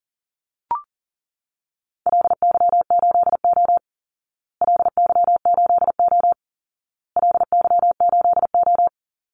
Morse code, heard as a keyed tone, about 40 wpm, 700 Hz. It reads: LY8O LY8O LY8O
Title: Call Signs Repeated 2x - 40wpm